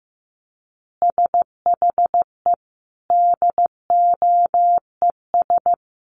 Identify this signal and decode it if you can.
Morse code, heard as a keyed tone, about 15 wpm, 700 Hz. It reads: SHE DOES
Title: Additional Easy Phrases 15wpm